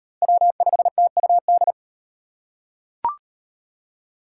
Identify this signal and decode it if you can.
Morse code, heard as a keyed tone, about 33 wpm, 700 Hz. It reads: W5TUD